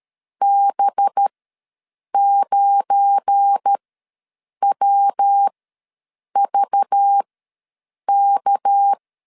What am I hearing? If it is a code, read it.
Morse code, heard as a keyed tone, about 13 wpm, spaced as for 7 wpm, 800 Hz. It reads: B9WVK
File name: LEZIONI / 9 286 ws 12wpm 3351